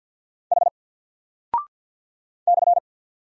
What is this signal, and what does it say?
Morse code, heard as a keyed tone, about 48 wpm, 700 Hz. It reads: H /
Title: Single Character (Letter:Number:Punctuation:Prosign) Alt 50wpm